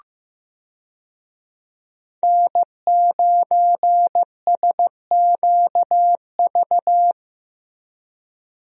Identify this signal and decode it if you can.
Morse code, heard as a keyed tone, about 15 wpm, 700 Hz. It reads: N9SQV
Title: USA Call Signs - 5 Characters - Repeated 2x - 15wpm